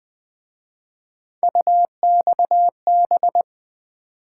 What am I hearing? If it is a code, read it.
Morse code, heard as a keyed tone, about 24 wpm, 700 Hz. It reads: UXB
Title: Speed-Racing - Three Letter-Number Combinations - 20wpm